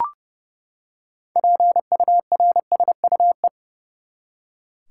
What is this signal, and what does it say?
Morse code, heard as a keyed tone, about 32 wpm, 700 Hz. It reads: PURSUE
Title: Six Letter Words - 30wpm